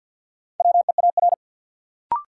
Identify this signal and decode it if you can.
Morse code, heard as a keyed tone, about 53 wpm, 700 Hz. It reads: WEAR
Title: Sentences from Top 1000 Words - Review Part 6 - New Word Drill - 50wpm